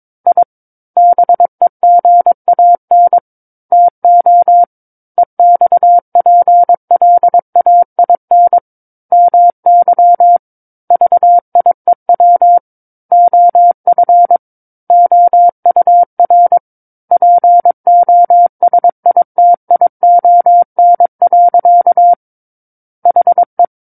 Morse code, heard as a keyed tone, about 22 wpm, 700 Hz. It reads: I BEGAN TO EXPLAIN MY VIEW OF OUR POSITION. HE
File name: Book One / 13-Chapter 13 - 22 WPM